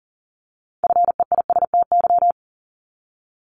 Morse code, heard as a keyed tone, about 38 wpm, 700 Hz. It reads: FEISTY
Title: Multi-letter Phonemes Repeated 3x 40wpm